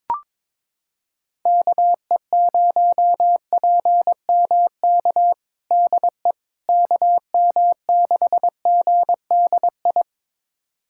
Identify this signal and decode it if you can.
Morse code, heard as a keyed tone, about 22 wpm, 700 Hz. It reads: KE0PMK DE KM6GDI
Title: USA Callsign DE Callsign - 6 Characters - 22wpm